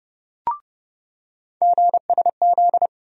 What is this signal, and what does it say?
Morse code, heard as a keyed tone, about 30 wpm, 700 Hz. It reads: GSZ
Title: USA Call Sign Postfixes - 3 Letters 30wpm